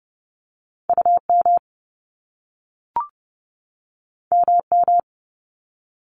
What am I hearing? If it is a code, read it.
Morse code, heard as a keyed tone, about 30 wpm, 700 Hz. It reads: UM MM
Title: Top Two Letter Combinations in Words 30wpm